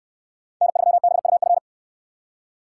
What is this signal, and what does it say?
Morse code, heard as a keyed tone, about 69 wpm, 700 Hz. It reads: N3BUF